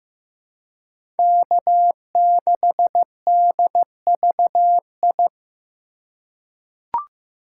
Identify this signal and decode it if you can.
Morse code, heard as a keyed tone, about 15 wpm, 700 Hz. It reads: K6DVI
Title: USA Call Signs - 5 Characters - 3x Character Spacing - 15wpm